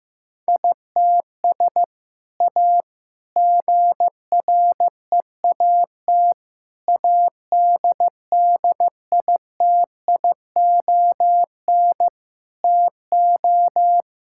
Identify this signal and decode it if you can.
Morse code, heard as a keyed tone, about 15 wpm, 700 Hz. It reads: ITS A GREAT ADDITION TO
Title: Sentences from Top 1000 Words - Review Part 6 - 15wpm